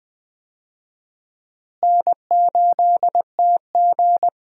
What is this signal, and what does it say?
Morse code, heard as a keyed tone, about 21 wpm, 700 Hz. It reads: N8TG
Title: USA Call Signs - 4 Characters - Repeated 3x - 20wpm